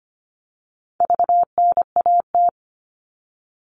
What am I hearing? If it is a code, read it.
Morse code, heard as a keyed tone, about 25 wpm, 700 Hz. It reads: VNAT
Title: All Cut Numbers - 4-Digit 25wpm